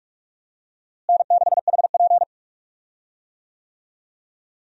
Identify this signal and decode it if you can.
Morse code, heard as a keyed tone, about 45 wpm, 700 Hz. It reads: N6HP
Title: USA Call Signs - 4 Characters - Repeated 2x - 45wpm